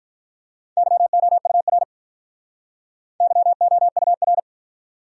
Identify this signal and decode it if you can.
Morse code, heard as a keyed tone, about 44 wpm, 700 Hz. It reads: YOUR YOUR